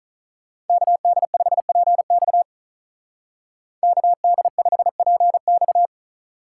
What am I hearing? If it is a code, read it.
Morse code, heard as a keyed tone, about 45 wpm, 700 Hz. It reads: KD5PX KD5PX